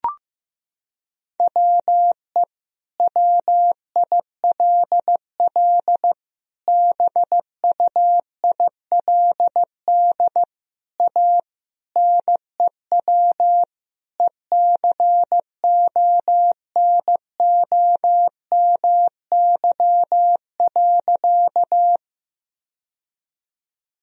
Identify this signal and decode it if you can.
Morse code, heard as a keyed tone, about 15 wpm, 700 Hz. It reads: WE WILL BUILD A NEW ECONOMY.
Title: Sentences from Top 1000 Words - Review Part 3 - 15wpm